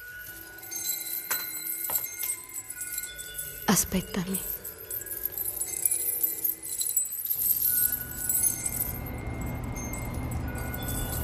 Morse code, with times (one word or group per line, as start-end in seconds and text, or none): none